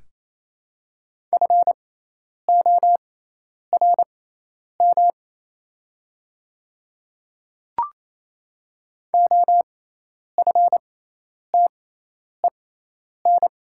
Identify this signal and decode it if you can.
Morse code, heard as a keyed tone, about 28 wpm, spaced as for 10 wpm, 700 Hz. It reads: FORM OFTEN
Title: 6x Character Spacing - Top 300 Words - 28wpm